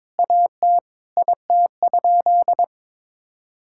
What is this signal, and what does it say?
Morse code, heard as a keyed tone, about 21 wpm, 700 Hz. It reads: AT IT?
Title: Sentences from Top 1600 Words 22wpm